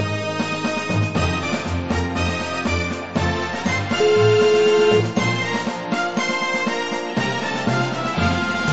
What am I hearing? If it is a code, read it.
Ringback tone (Europe, 425 Hz), with noise.